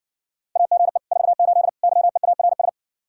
Morse code, heard as a keyed tone, about 60 wpm, 700 Hz. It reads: AGE 48 YEARS